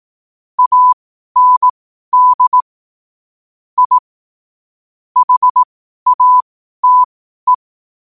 Morse code, heard as a keyed tone, about 18 wpm, spaced as for 12 wpm, 1000 Hz. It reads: AND I HATE
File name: qotdfiles / 12wpm